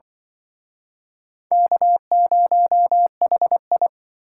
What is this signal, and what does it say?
Morse code, heard as a keyed tone, about 24 wpm, 700 Hz. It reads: K0HI
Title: Speed Racing - USA 1x2 Call Signs 20wpm